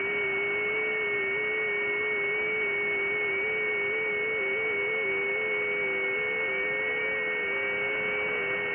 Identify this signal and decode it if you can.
FT8.
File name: IC-7100